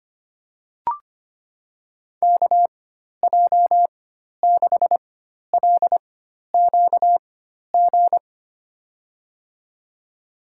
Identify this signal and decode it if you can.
Morse code, heard as a keyed tone, about 25 wpm, spaced as for 12 wpm, 700 Hz. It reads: KJ6LQG
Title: USA Call Signs - 6 Characters - 4x Character Spacing - 25wpm